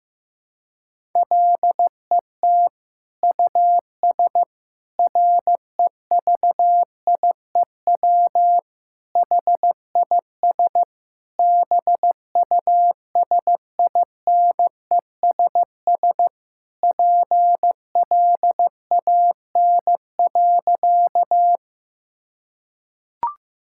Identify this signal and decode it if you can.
Morse code, heard as a keyed tone, about 15 wpm, 700 Hz. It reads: LET US REVIEW HIS BUSINESS PLAN.
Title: Sentences from Top 1000 Words - Review Part 6 - 15wpm